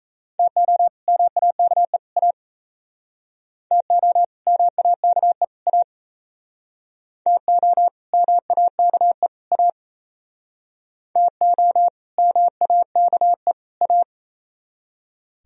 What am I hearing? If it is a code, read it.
Morse code, heard as a keyed tone, about 38 wpm, 700 Hz. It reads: TO MAKE A TO MAKE A TO MAKE A TO MAKE A